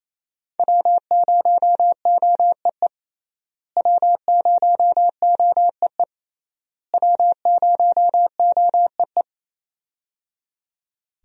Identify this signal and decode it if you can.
Morse code, heard as a keyed tone, about 28 wpm, 700 Hz. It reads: W0OEE W0OEE W0OEE
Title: USA Call Signs - 5 Characters - Repeated 3x - 28wpm